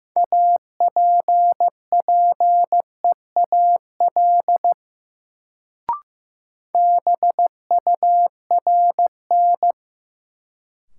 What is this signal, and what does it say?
Morse code, heard as a keyed tone, about 16 wpm, 700 Hz. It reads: APPEAL BURN
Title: Sentences from Top 1300 Words - New Word Drill - 15wpm